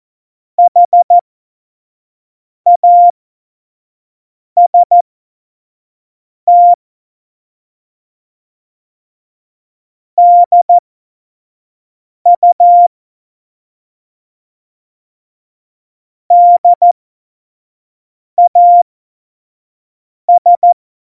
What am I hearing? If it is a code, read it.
Morse code, heard as a keyed tone, about 14 wpm, spaced as for 5 wpm, 700 Hz. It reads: HAST DU DAS